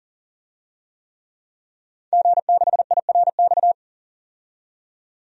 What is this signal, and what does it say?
Morse code, heard as a keyed tone, about 40 wpm, 700 Hz. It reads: G6IRX
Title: DXCC-100 Call Signs - 5 Characters - 40wpm